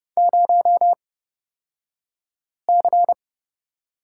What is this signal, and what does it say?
Morse code, heard as a keyed tone, about 31 wpm, spaced as for 5 wpm, 700 Hz. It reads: 0C
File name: Lesson 38d - Letter q - Two Characters 30wpm